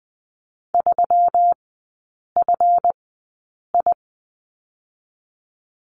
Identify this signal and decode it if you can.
Morse code, heard as a keyed tone, about 20 wpm, spaced as for 8 wpm, 700 Hz. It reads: 3FI
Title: 138 Lesson 23e - Letter f - Three Characters 20wpm